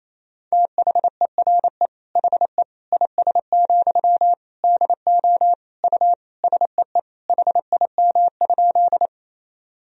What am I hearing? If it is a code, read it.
Morse code, heard as a keyed tone, about 28 wpm, 700 Hz. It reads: THERE HE IS, DO U SEE HIM?
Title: Sentences from Top 100 Words and Abbreviations 28wpm